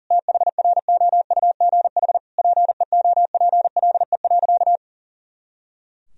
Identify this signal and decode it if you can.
Morse code, heard as a keyed tone, about 40 wpm, 700 Hz. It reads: THROUGH PEOPLE.
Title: Sentences from Top 200 Words and Abbreviations 40wpm